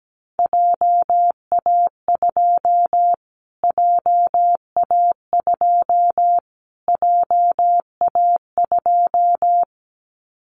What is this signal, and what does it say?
Morse code, heard as a keyed tone, about 17 wpm, 700 Hz. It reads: JA2 JA2 JA2
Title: Callsign Prefixes for Top 25 DXCC - Repeated 3x - 17wpm